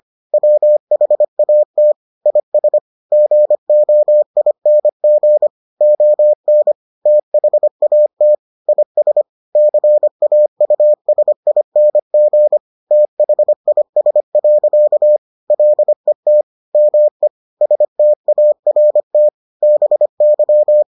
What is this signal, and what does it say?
Morse code, heard as a keyed tone, about 25 wpm, 600 Hz. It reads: WHAT IS GOING ON THAT IS CAUSING THIS. LET ME START BY